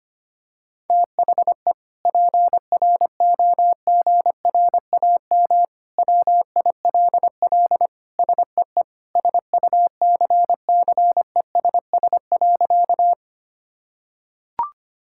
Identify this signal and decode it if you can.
Morse code, heard as a keyed tone, about 25 wpm, 700 Hz. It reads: THE PROGRAM WILL SEE SUCCESS.
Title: Sentences from Top 1000 Words - Review Part 5 - 25wpm